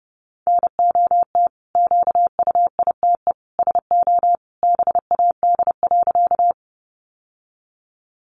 Morse code, heard as a keyed tone, about 30 wpm, 700 Hz. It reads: NOT QUITE SO BAD.
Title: Sentences from Top 700 Words 30wpm